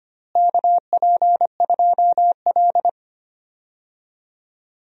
Morse code, heard as a keyed tone, about 25 wpm, 700 Hz. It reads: KP2L
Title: DXCC-100 Call Signs - 4 Characters - Repeated 3x - 25wpm